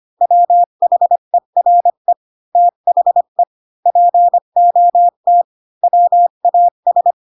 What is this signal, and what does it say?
Morse code, heard as a keyed tone, about 25 wpm, 700 Hz. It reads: WHERE THE POT WAS
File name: Pinocchio / Pinocchio ch5 20 25 ik1hge